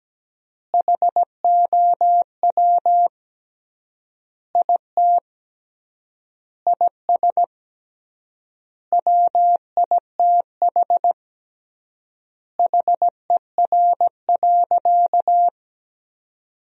Morse code, heard as a keyed tone, about 17 wpm, 700 Hz. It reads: HOW IT IS WITH HER.
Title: 3x Word-Spacing - Sentences from Top 100 Words - 17wpm